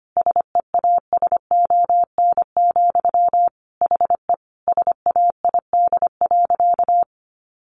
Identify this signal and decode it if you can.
Morse code, heard as a keyed tone, about 25 wpm, 700 Hz. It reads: SEASON, HE SAID.
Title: Sentences from Top 1400 Words 25wpm